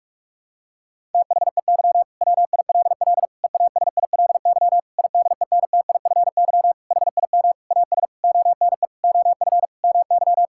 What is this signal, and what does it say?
Morse code, heard as a keyed tone, about 46 wpm, 700 Hz. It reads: THEY WILL EASILY IDENTIFY HIM AS ONE OF MY
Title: Sentences from Top 1000 Words - Review Part 6 - 45wpm